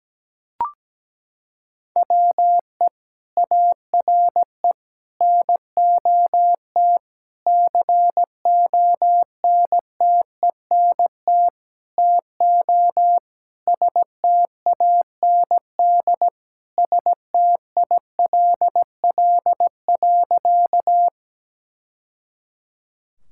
Morse code, heard as a keyed tone, about 17 wpm, 700 Hz. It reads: WE ARE NOT CONTENT TO STAND STILL.
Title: Sentences from Top 1000 Words - Review Part 6 - 17wpm